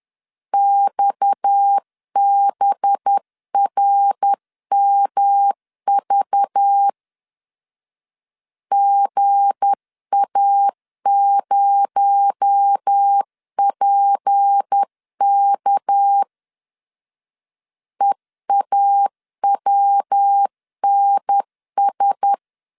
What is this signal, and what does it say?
Morse code, heard as a keyed tone, about 10 wpm, 800 Hz. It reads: XBRMV GA0PK EAWNS
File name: LEZIONI / 7 GF9Z / 7 G 10wpm 2501